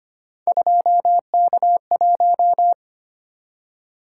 Morse code, heard as a keyed tone, about 25 wpm, 700 Hz. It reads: 2K1
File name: Three Letter-Number Combinations - 25wpm